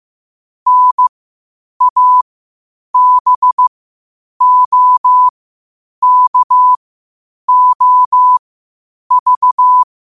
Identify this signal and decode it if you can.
Morse code, heard as a keyed tone, about 15 wpm, spaced as for 8 wpm, 1000 Hz. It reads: NABOKOV
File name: qotdfiles / 8wpm